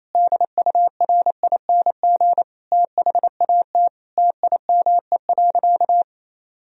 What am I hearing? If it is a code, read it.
Morse code, heard as a keyed tone, about 28 wpm, 700 Hz. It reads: DURING THAT TIME.
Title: Sentences from Top 1000 Words - Review Part 2 - 28wpm